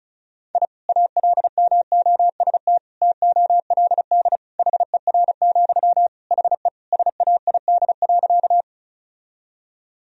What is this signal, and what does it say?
Morse code, heard as a keyed tone, about 34 wpm, 700 Hz. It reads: I ALMOST TOLD HER, HE SAID.
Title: Sentences from Top 1000 Words - Review Part 4 - 35wpm